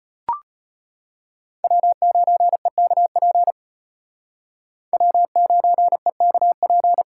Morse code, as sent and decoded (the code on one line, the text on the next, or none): .-- ----. -.- .--. / .-- ----. -.- .--.
W9EKP W9EKP